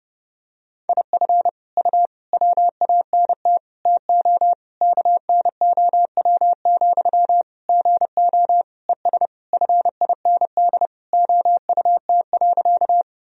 Morse code, heard as a keyed tone, about 30 wpm, 700 Hz. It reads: IF U WANT TO KNOW, GO ES FIND OUT.
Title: Sentences from Top 200 Words and Abbreviations 30wpm